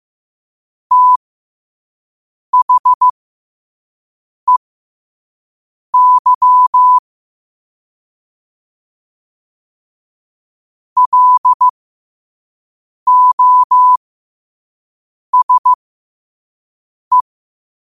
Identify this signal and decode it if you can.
Morse code, heard as a keyed tone, about 15 wpm, spaced as for 5 wpm, 1000 Hz. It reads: THEY LOSE
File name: qotdfiles / 5wpm